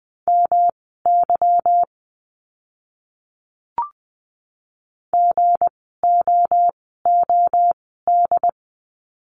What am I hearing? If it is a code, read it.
Morse code, heard as a keyed tone, about 20 wpm, spaced as for 14 wpm, 700 Hz. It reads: MY GOOD
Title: 230 Lesson 37f - Letter x - Words 20wpm